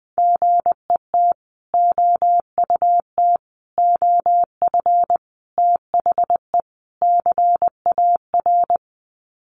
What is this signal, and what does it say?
Morse code, heard as a keyed tone, about 20 wpm, 700 Hz. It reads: GET OUT OF THE CAR
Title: Sets of 5 Words Encore2 20wpm